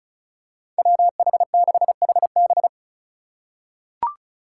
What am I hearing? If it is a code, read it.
Morse code, heard as a keyed tone, about 34 wpm, 700 Hz. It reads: WH6HB
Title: USA Call Signs - 5 Characters - 2x Character Spacing - 35wpm